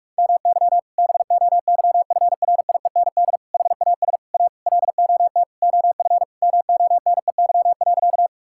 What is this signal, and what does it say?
Morse code, heard as a keyed tone, about 46 wpm, 700 Hz. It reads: MY BOYFRIEND HAS A LOT OF MONEY.
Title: Sentences from Top 1000 Words - Review Part 5 - 45wpm